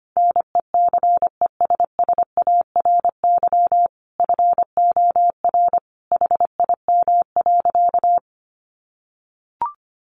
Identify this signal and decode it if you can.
Morse code, heard as a keyed tone, about 25 wpm, 700 Hz. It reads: NECESSARY FOR HIM.
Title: Sentences from Top 1500 Words 25wpm